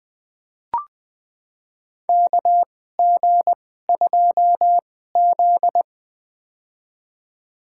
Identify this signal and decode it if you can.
Morse code, heard as a keyed tone, about 20 wpm, spaced as for 15 wpm, 700 Hz. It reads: KG2Z
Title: USA Call Signs - 2x Character Spacing - 20wpm